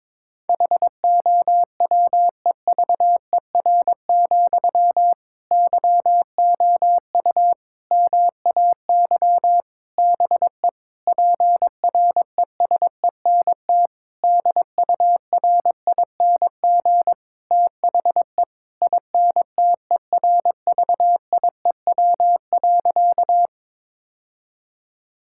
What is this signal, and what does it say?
Morse code, heard as a keyed tone, about 22 wpm, 700 Hz. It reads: HOWEVER, YOU MAY BE PRESENT DURING THE INTERVIEW.
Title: Sentences from Top 1600 Words 22wpm